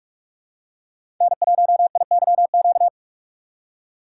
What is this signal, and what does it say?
Morse code, heard as a keyed tone, about 44 wpm, 700 Hz. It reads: N1IYQ
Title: USA Call Signs - 5 Characters - 8x Character Spacing - 45wpm